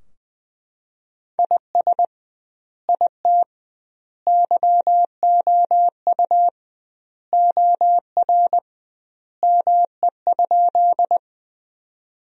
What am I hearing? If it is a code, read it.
Morse code, heard as a keyed tone, about 20 wpm, 700 Hz. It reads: IS IT YOU OR ME?
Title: 2x Word-Spacing - Sentences from Top 200 Words - 20wpm